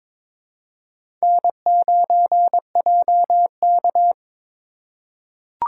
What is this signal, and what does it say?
Morse code, heard as a keyed tone, about 22 wpm, 700 Hz. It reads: N9JK
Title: USA Call Signs - 4 Characters - 4x Character Spacing - 22wpm